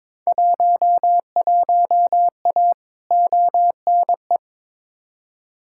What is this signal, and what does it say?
Morse code, heard as a keyed tone, about 21 wpm, 700 Hz. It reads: 11A ONE
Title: Field Day Exchanges Encore - 22wpm